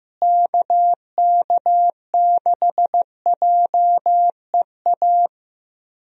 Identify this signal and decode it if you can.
Morse code, heard as a keyed tone, about 15 wpm, 700 Hz. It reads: KK6JEA
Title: USA Call Signs - 6 Characters - Repeated 2x - 15wpm